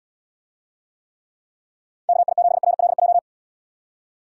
Morse code, heard as a keyed tone, about 77 wpm, 700 Hz. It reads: KI7RLJ